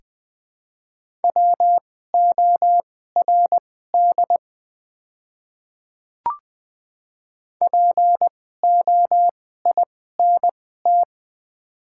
Morse code, heard as a keyed tone, about 20 wpm, spaced as for 14 wpm, 700 Hz. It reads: WORD POINT
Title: 2x Character Spacing - Top 200 Words - 20wpm